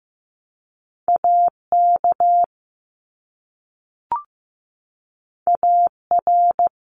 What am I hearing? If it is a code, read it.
Morse code, heard as a keyed tone, about 15 wpm, 700 Hz. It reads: AK AR
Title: US States 15wpm